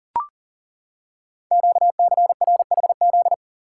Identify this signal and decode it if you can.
Morse code, heard as a keyed tone, about 39 wpm, 700 Hz. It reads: QCRHZ
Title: Code Groups - 5 Letters - 40wpm